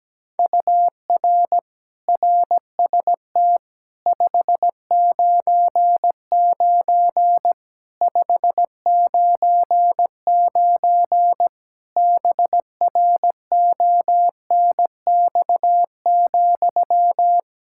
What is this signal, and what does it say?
Morse code, heard as a keyed tone, about 17 wpm, 700 Hz. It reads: UR RST 599 599 BRONX,
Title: Practice QSO - Exchange 1 - Beginning - 17wpm